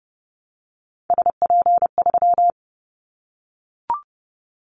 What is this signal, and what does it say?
Morse code, heard as a keyed tone, about 30 wpm, 700 Hz. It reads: SP3
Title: Callsign Prefixes for Top 25 DXCC - Repeated 3x - 30wpm